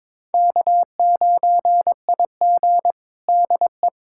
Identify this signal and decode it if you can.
Morse code, heard as a keyed tone, about 22 wpm, 700 Hz. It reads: K9IG DE
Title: USA Callsign DE Callsign - 4 Characters - 22wpm